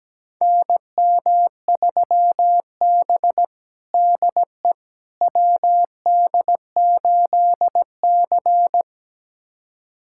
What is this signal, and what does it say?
Morse code, heard as a keyed tone, about 17 wpm, 700 Hz. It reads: NM3B DE WD8C
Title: USA Callsign DE Callsign - 4 Characters - 17wpm